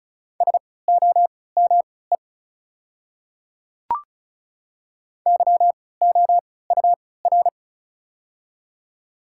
Morse code, heard as a keyed tone, about 35 wpm, spaced as for 20 wpm, 690 Hz. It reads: SOME YOUR
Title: 3x Character Spacing - Top 100 Words - 35wpm